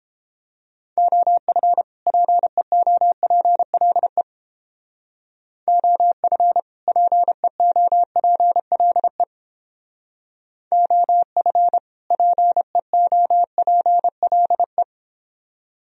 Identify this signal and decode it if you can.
Morse code, heard as a keyed tone, about 29 wpm, 700 Hz. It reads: OF PEOPLE OF PEOPLE OF PEOPLE